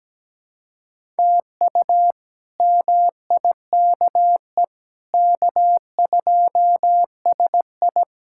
Morse code, heard as a keyed tone, about 17 wpm, 700 Hz. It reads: TU MIKE K2SI
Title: K1USN SST QSO Practice - 17wpm